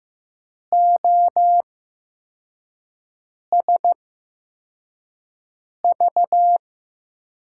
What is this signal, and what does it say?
Morse code, heard as a keyed tone, about 15 wpm, spaced as for 4 wpm, 700 Hz. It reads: OSV
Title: USA Call Signs - 5 Characters - 8x Character Spacing - 15wpm